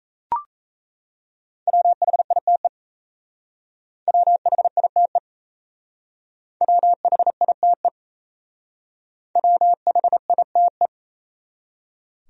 Morse code, heard as a keyed tone, about 34 wpm, 700 Hz. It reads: WHITE WHITE WHITE WHITE
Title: Speed-Racing Top 300 Words 28wpm